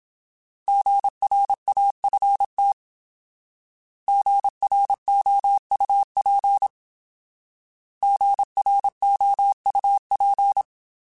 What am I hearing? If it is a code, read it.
Morse code, heard as a keyed tone, about 27 wpm, 800 Hz. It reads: GRAFT GROUP GROUP
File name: Alphabetical / Alphabetical mp3s / Gs @25